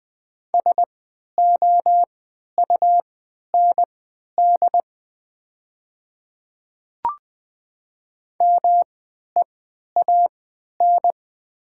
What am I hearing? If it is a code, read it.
Morse code, heard as a keyed tone, about 20 wpm, spaced as for 11 wpm, 700 Hz. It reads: SOUND MEAN
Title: 3x Character Spacing - Top 300 Words - 20wpm